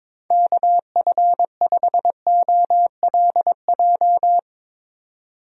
Morse code, heard as a keyed tone, about 22 wpm, 700 Hz. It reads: KF5OLJ
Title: USA Call Signs - 6 Characters - 22wpm